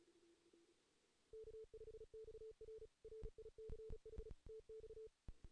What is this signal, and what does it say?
Morse code, heard as a keyed tone, about 36 wpm, 440 Hz. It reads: K5XR RICH TX
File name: 2016 JAN CW NAQP / 2016 Jan naqp k5ot - 0-3